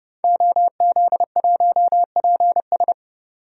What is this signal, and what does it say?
Morse code, heard as a keyed tone, about 32 wpm, 700 Hz. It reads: OZ1PS
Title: DXCC-100 Call Signs - 5 Characters - 30wpm